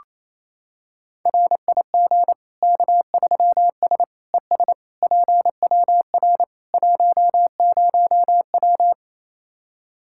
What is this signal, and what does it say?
Morse code, heard as a keyed tone, about 28 wpm, 700 Hz. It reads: RIG K3S ES PWR 10W
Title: Practice QSO - Exchange 2 - Info - 28wpm